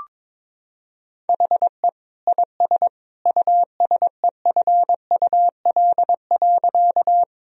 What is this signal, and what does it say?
Morse code, heard as a keyed tone, about 22 wpm, 700 Hz. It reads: HE IS USEFUL.
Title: Sentences from Top 1000 Words - Review Part 4 - 22wpm